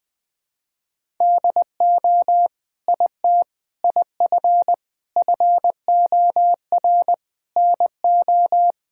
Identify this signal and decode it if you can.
Morse code, heard as a keyed tone, about 20 wpm, 700 Hz. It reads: DO IT IF FOR NO
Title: Sentences with 7-Word Phrases - 20wpm